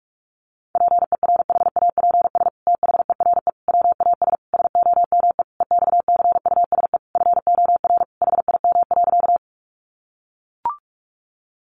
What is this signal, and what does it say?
Morse code, heard as a keyed tone, about 44 wpm, 700 Hz. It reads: PERHAPS THERE WAS SOME EXCUSE FOR HIM.
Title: Sentences from Top 700 Words 45wpm